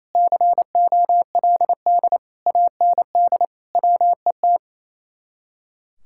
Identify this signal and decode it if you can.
Morse code, heard as a keyed tone, about 36 wpm, 700 Hz. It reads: COLD AND WET